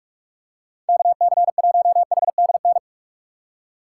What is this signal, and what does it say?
Morse code, heard as a keyed tone, about 46 wpm, 700 Hz. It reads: KC1HDN
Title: USA Call Signs - 2x Character Spacing - 45wpm